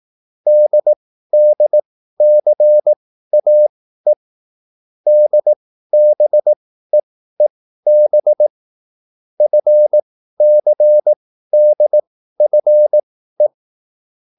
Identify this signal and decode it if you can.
Morse code, heard as a keyed tone, about 18 wpm, spaced as for 13 wpm, 600 Hz. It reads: DDCAE DBEEB FCDFE
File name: QSL / CW lesson